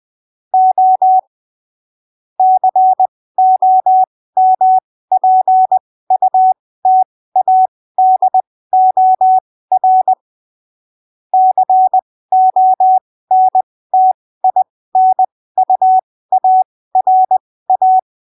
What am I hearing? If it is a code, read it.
Morse code, heard as a keyed tone, about 20 wpm, spaced as for 15 wpm, 750 Hz. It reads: O COMPUTADOR CONTINUARA
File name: arquivos / texto / texto 25 20ppm